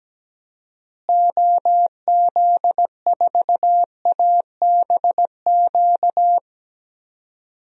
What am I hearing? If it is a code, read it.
Morse code, heard as a keyed tone, about 17 wpm, 700 Hz. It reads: OZ4ABQ
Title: DXCC-100 Call Signs - 6 Characters - Repeated 2x - 17wpm